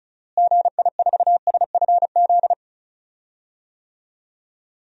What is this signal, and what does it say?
Morse code, heard as a keyed tone, about 34 wpm, 700 Hz. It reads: GI4SFZ
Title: DXCC-100 Call Signs - 6 Characters - Repeated 2x - 35wpm